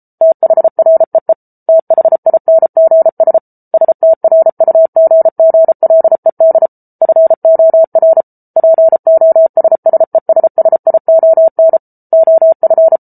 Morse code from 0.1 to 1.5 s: THREE

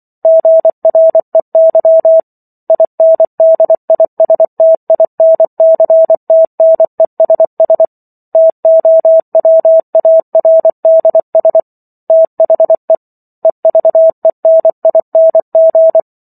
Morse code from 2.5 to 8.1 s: INDISTINCTNESS

